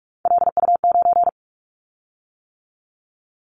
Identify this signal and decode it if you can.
Morse code, heard as a keyed tone, about 46 wpm, 700 Hz. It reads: LU9
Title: Callsign Prefixes for Top 50 DXCC - 45wpm